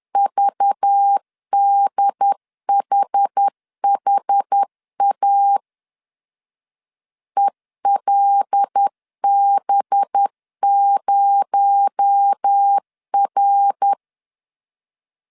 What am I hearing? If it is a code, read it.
Morse code, heard as a keyed tone, about 10 wpm, 800 Hz. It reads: VDHHA ELB0R
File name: CW / DET / 1 6 1500C S C 10wpm 4252